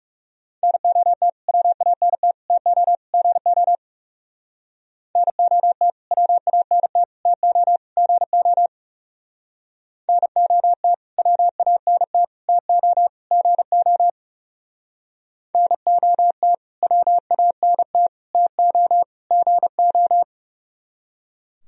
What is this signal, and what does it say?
Morse code, heard as a keyed tone, about 38 wpm, 700 Hz. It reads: NOT WANT TO GO NOT WANT TO GO NOT WANT TO GO NOT WANT TO GO